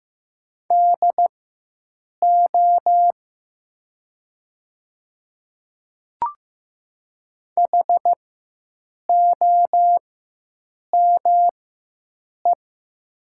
Morse code, heard as a keyed tone, about 15 wpm, spaced as for 7 wpm, 700 Hz. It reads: DO HOME
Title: 4x Character Spacing - Top 200 Words - 15wpm